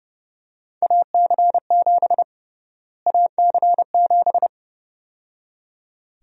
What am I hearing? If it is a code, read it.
Morse code, heard as a keyed tone, about 31 wpm, 700 Hz. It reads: AC7 AC7
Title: USA Call Sign Prefixes - Repeated 2x - 30wpm